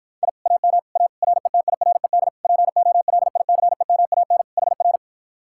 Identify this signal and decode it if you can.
Morse code, heard as a keyed tone, about 54 wpm, 700 Hz. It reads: I AM A RETIRED POLICEMAN HR